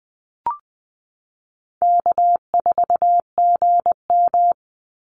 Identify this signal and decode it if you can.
Morse code, heard as a keyed tone, about 19 wpm, 700 Hz. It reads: K4GM
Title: Easy Call Signs Repeated 3x - 20wpm